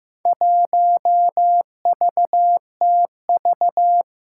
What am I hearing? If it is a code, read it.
Morse code, heard as a keyed tone, about 15 wpm, 700 Hz. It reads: 1VTV
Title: Four Challenging Letter-Number Combinations 15wpm